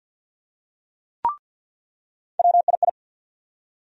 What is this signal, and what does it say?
Morse code, heard as a keyed tone, about 47 wpm, 700 Hz. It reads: WII
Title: USA Call Sign Postfixes - 3 Letters 50wpm